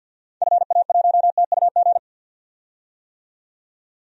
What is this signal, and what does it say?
Morse code, heard as a keyed tone, about 48 wpm, 700 Hz. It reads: FA1TUG
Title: DXCC-100 Call Signs - 6 Characters - 50wpm